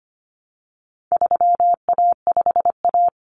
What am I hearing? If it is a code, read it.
Morse code, heard as a keyed tone, about 25 wpm, 700 Hz. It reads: 3A5A
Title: Common Cut Numbers - 4-Digit 25wpm